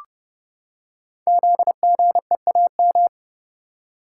Morse code, heard as a keyed tone, about 30 wpm, 700 Hz. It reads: ZGEAM
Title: Code Groups - 5 Letters - 30wpm